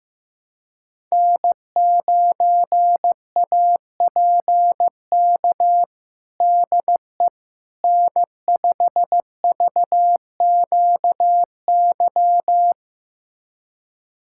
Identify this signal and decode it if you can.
Morse code, heard as a keyed tone, about 15 wpm, 700 Hz. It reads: N9APK DE N5VQY
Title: USA Callsign DE Callsign - 5 Characters - 15wpm